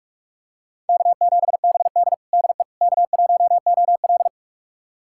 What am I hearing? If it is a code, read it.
Morse code, heard as a keyed tone, about 46 wpm, 700 Hz. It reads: K7BD DE K1OL
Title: USA Callsign DE Callsign - 4 Characters - 45wpm